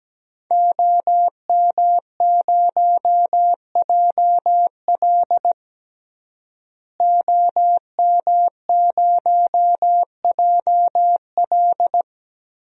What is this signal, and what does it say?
Morse code, heard as a keyed tone, about 17 wpm, 700 Hz. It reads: OM0JL OM0JL
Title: DXCC-100 Call Signs - 5 Characters - Repeated 3x - 17wpm